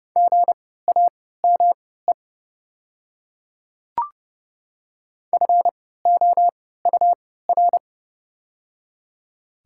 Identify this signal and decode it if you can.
Morse code, heard as a keyed tone, about 30 wpm, spaced as for 17 wpm, 700 Hz. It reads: GAME FOUR
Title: 3x Character Spacing - Top 500 Words - 30wpm